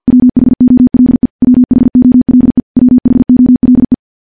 Morse code, heard as a keyed tone, about 50 wpm, 250 Hz. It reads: WHOLE WHOLE WHOLE
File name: morse / morse-training-50WPM